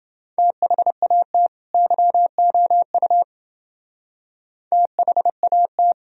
Morse code, heard as a keyed tone, about 28 wpm, 700 Hz. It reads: THAT YOU THAT